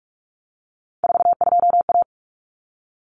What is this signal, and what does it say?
Morse code, heard as a keyed tone, about 44 wpm, 700 Hz. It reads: 42A
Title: Common Cut Numbers - 3-Digit 45wpm